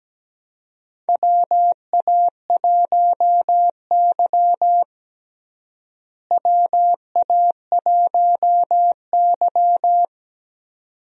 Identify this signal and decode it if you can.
Morse code, heard as a keyed tone, about 17 wpm, 700 Hz. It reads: WA1Y WA1Y
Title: USA Call Signs - 4 Characters - Repeated 2x - 17wpm